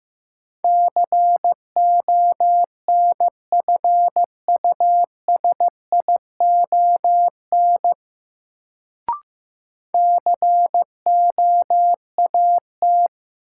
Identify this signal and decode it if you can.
Morse code, heard as a keyed tone, about 15 wpm, 700 Hz. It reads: CONFUSION COAT
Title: Sentences from Top 1600 Words - New Word Drill - 15wpm